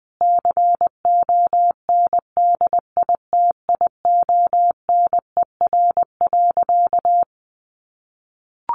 Morse code, heard as a keyed tone, about 20 wpm, 700 Hz. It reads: CONDITIONER.
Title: Easy Phrases 20wpm